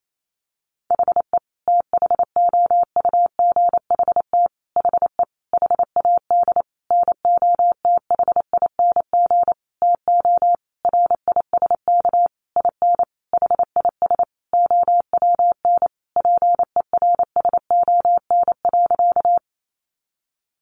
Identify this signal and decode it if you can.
Morse code, heard as a keyed tone, about 28 wpm, 700 Hz. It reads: HE THOUGHT HE HAD NOTHING TO RISK IN HIS OWN PERSON.
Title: Sentences from Top 600 Words 28wpm